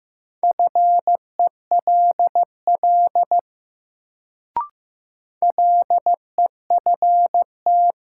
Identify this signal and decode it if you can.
Morse code, heard as a keyed tone, about 15 wpm, 700 Hz. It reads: FELL LEFT
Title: Sets of 5 Words Encore - New Word Drill - 15wpm